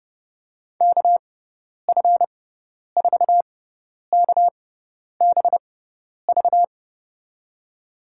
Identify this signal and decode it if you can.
Morse code, heard as a keyed tone, about 30 wpm, spaced as for 10 wpm, 700 Hz. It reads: KF4KBV
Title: USA Call Signs - 6 Characters - 6x Character Spacing - 30wpm